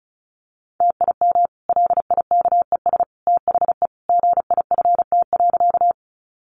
Morse code, heard as a keyed tone, about 34 wpm, 700 Hz. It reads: TIM LIKES THE GIFT.
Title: Easy Phrases 35wpm